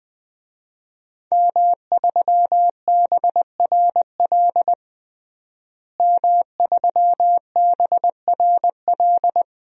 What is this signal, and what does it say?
Morse code, heard as a keyed tone, about 21 wpm, 700 Hz. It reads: M3BRL M3BRL
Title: DXCC-100 Call Signs - 5 Characters - Repeated 3x - 20wpm